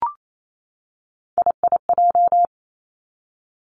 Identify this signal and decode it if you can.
Morse code, heard as a keyed tone, about 28 wpm, 700 Hz. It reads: IIJ
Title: Three Challenging Letter-Number Combinations 28wpm